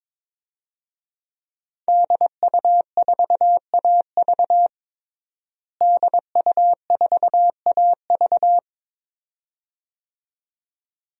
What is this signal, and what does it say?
Morse code, heard as a keyed tone, about 22 wpm, 700 Hz. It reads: DU4AV DU4AV
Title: DXCC-100 Call Signs - 5 Characters - Repeated 2x - 22wpm